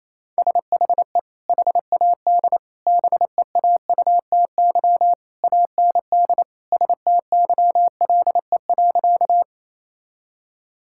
Morse code, heard as a keyed tone, about 28 wpm, 700 Hz. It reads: SHE HAD BEAUTY AND STYLE.
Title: Sentences from Top 1000 Words - Review Part 5 - 28wpm